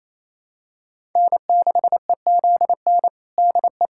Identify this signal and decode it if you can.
Morse code, heard as a keyed tone, about 28 wpm, 700 Hz. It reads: N6EZN DE
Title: USA Callsign DE Callsign - 5 Characters - 28wpm